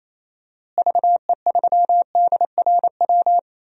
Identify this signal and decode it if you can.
Morse code, heard as a keyed tone, about 28 wpm, 700 Hz. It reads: VE3DRW
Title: DXCC-100 Call Signs - 6 Characters - Repeated 3x - 28wpm